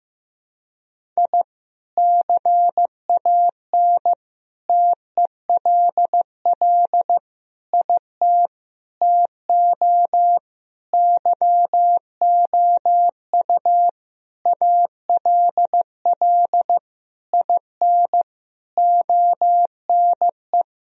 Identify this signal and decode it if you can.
Morse code, heard as a keyed tone, about 16 wpm, 700 Hz. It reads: I CAN TELL IT TO YOU ALL IN ONE
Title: Sentences from Top 1000 Words - Review Part 1 - 15wpm